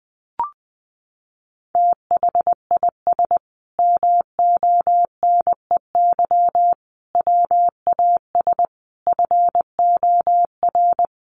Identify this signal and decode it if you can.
Morse code, heard as a keyed tone, about 20 wpm, 700 Hz. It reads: THIS MONEY WAS FOR
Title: Sentences from Top 400 Words 20wpm